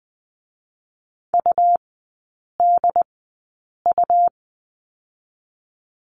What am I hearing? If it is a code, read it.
Morse code, heard as a keyed tone, about 20 wpm, spaced as for 8 wpm, 700 Hz. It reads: UDU
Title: 107 Lesson 18e - Letter m - Three Characters 20wpm